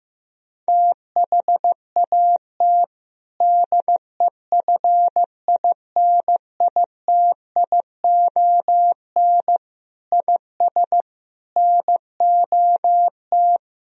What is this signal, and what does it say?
Morse code, heard as a keyed tone, about 15 wpm, 700 Hz. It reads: THAT DEFINITION IS NOT
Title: Sentences from Top 1000 Words - Review Part 4 - 15wpm